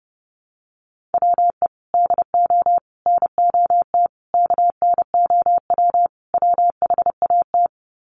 Morse code, heard as a keyed tone, about 30 wpm, 700 Hz. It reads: WE DO NOT KNOW WHAT
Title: Sets of 5 Words 30wpm